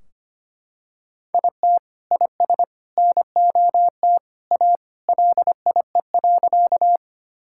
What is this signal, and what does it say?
Morse code, heard as a keyed tone, about 25 wpm, 700 Hz. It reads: IT IS NOT A LIE.
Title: Sentences from Top 1600 Words 25wpm